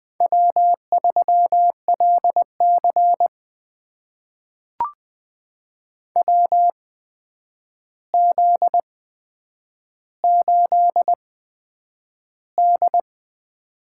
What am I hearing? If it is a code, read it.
Morse code, heard as a keyed tone, about 20 wpm, spaced as for 9 wpm, 700 Hz. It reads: W3LC WZ8D
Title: USA Call Signs - 4 Characters - 8x Character Spacing - 20wpm